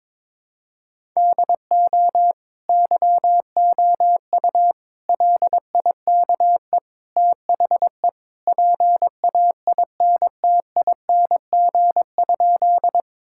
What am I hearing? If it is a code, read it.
Morse code, heard as a keyed tone, about 22 wpm, 700 Hz. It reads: DO YOU LIKE THE PAINTING?
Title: Sentences from Top 1000 Words - Review Part 6 - 22wpm